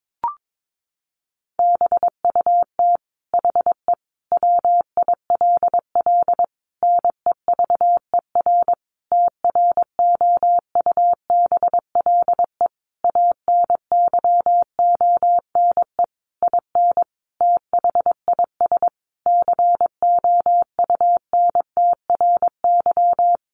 Morse code, heard as a keyed tone, about 22 wpm, 700 Hz. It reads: BUT HE WILL NEVER TROUBLE ANYONE IN THIS COUNTRY
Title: Sentences from Top 1200 Words 22wpm